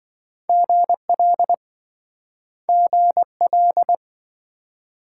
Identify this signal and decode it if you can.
Morse code, heard as a keyed tone, about 28 wpm, 700 Hz. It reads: GL GL